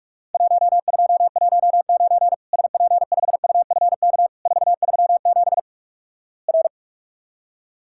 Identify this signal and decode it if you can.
Morse code, heard as a keyed tone, about 46 wpm, 700 Hz. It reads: 1219 SP5UFK 437 R
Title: Worked All Europe DX Contest - QTC Exchange - 45wpm